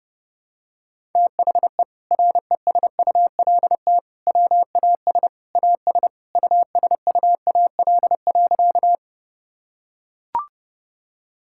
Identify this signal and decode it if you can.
Morse code, heard as a keyed tone, about 32 wpm, 700 Hz. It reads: THE RESULT WAS AS USUAL.
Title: Sentences from Top 1000 Words - Review Part 5 - 30wpm